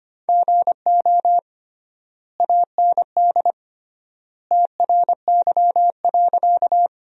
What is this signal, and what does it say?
Morse code, heard as a keyed tone, about 25 wpm, 700 Hz. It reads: GO AND TRY.
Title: 3x Word-Spacing - Sentences from Top 200 Words - 25wpm